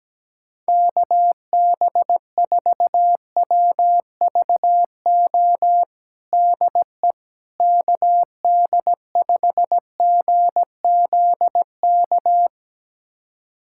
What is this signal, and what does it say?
Morse code, heard as a keyed tone, about 17 wpm, 700 Hz. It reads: KB4WVO DE KD5GZK
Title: USA Callsign DE Callsign 17wpm